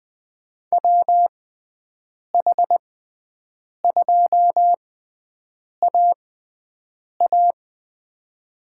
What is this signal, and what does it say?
Morse code, heard as a keyed tone, about 20 wpm, spaced as for 7 wpm, 700 Hz. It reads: WH2AA
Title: USA Call Signs - 5 Characters - 6x Character Spacing - 20wpm